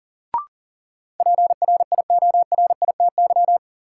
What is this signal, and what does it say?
Morse code, heard as a keyed tone, about 38 wpm, 700 Hz. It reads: PRIORITY
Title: Sentences from Top 1600 Words - New Word Drill - 40wpm